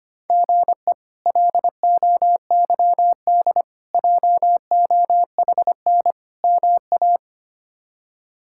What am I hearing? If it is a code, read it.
Morse code, heard as a keyed tone, about 25 wpm, 700 Hz. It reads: GE LOYD JOHN MA
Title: K1USN SST QSO Practice - 25wpm